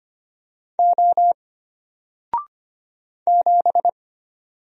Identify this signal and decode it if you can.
Morse code, heard as a keyed tone, about 25 wpm, 700 Hz. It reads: O 7
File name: Single-Letter-Number-25wpm